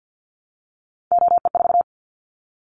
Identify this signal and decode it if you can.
Morse code, heard as a keyed tone, about 50 wpm, 700 Hz. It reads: OE4
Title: Callsign Prefixes for Top 100 DXCC - 50wpm